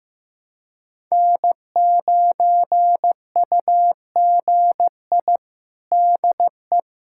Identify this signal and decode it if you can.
Morse code, heard as a keyed tone, about 16 wpm, 700 Hz. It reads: N9UGI DE
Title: USA Callsign DE Callsign - 5 Characters - 15wpm